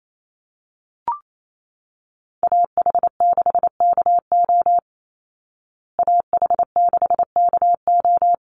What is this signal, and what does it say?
Morse code, heard as a keyed tone, about 28 wpm, 700 Hz. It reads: AH6KO AH6KO
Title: Call Signs Repeated 3x - 28wpm